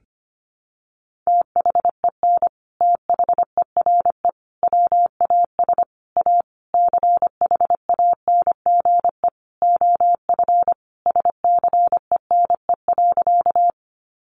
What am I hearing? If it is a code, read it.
Morse code, heard as a keyed tone, about 25 wpm, 700 Hz. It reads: THEN THERE WAS A CHANGE OF SCENE.
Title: Sentences from Top 1100 Words 25wpm